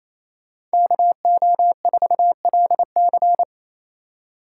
Morse code, heard as a keyed tone, about 35 wpm, 700 Hz. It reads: KO4LC